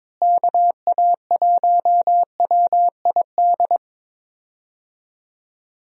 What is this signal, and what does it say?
Morse code, heard as a keyed tone, about 22 wpm, 700 Hz. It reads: KA1WID
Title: USA Call Signs - 6 Characters - Repeated 3x - 22wpm